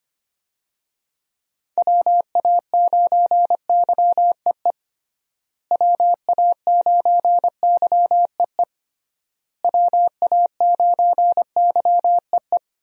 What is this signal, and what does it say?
Morse code, heard as a keyed tone, about 25 wpm, 700 Hz. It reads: WA9YEE WA9YEE WA9YEE
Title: USA Call Signs - 6 Characters - Repeated 3x - 25wpm